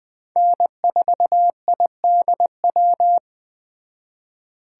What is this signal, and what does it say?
Morse code, heard as a keyed tone, about 20 wpm, 700 Hz. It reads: N4IDW
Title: Speed Racing - USA 1x3 Call Signs 17wpm